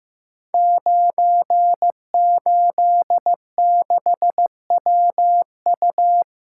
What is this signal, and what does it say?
Morse code, heard as a keyed tone, about 15 wpm, 700 Hz. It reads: 986WU
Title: Code Groups - 5 Letter:Numbers - 15wpm